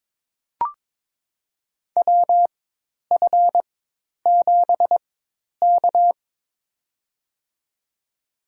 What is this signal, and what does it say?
Morse code, heard as a keyed tone, about 22 wpm, spaced as for 10 wpm, 700 Hz. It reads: WF7K
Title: USA Call Signs - 4 Characters - 4x Character Spacing - 22wpm